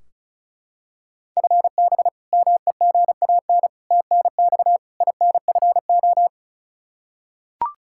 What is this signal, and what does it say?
Morse code, heard as a keyed tone, about 36 wpm, 700 Hz. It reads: FB MEGAN TNX INFO
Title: Practice QSO - Exchange 3 - More Info - 35wpm